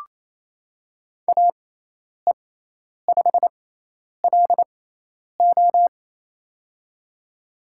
Morse code, heard as a keyed tone, about 28 wpm, spaced as for 10 wpm, 700 Hz. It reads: AE5LO
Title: USA Call Signs - 5 Characters - 6x Character Spacing - 28wpm